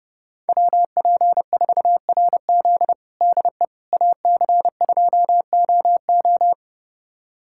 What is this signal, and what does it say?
Morse code, heard as a keyed tone, about 30 wpm, 700 Hz. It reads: WP4RZ DE AC2OO
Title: USA Callsign DE Callsign - 5 Characters - 30wpm